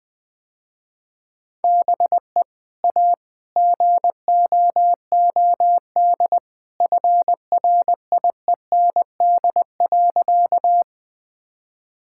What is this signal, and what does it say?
Morse code, heard as a keyed tone, about 20 wpm, 700 Hz. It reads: BE A GOOD FRIEND.
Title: Sentences with 2-Word Phrases - Spoken Beginning - 20wpm